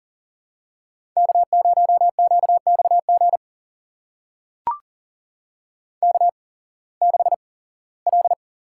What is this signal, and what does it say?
Morse code, heard as a keyed tone, about 39 wpm, spaced as for 21 wpm, 700 Hz. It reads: K0QXG K6L
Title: USA Call Signs - 5 Characters - 8x Character Spacing - 40wpm